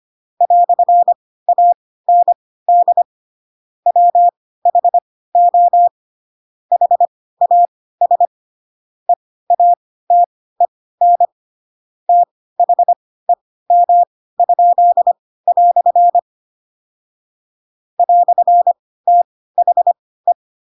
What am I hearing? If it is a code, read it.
Morse code, heard as a keyed tone, about 25 wpm, spaced as for 16 wpm, 700 Hz. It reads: "AND WHO HAS EATEN THEM?" "THE
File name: Pinocchio / Pinocchio ch7 10 25 ik1hge